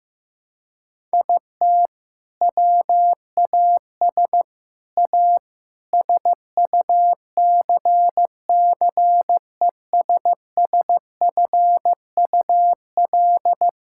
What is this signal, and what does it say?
Morse code, heard as a keyed tone, about 15 wpm, 700 Hz. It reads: IT WAS A SUCCESSFUL
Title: Sentences from Top 1000 Words - Review Part 6 - 15wpm